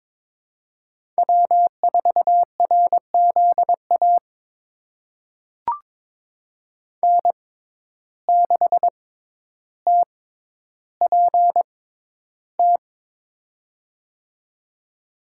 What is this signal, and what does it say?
Morse code, heard as a keyed tone, about 21 wpm, spaced as for 11 wpm, 700 Hz. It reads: W4RZA N6TPT
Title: USA Call Signs - 5 Characters - 6x Character Spacing - 22wpm